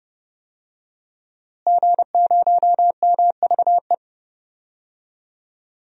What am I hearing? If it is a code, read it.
Morse code, heard as a keyed tone, about 31 wpm, 700 Hz. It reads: G0MVE
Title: DXCC-100 Call Signs - 5 Characters - Repeated 3x - 30wpm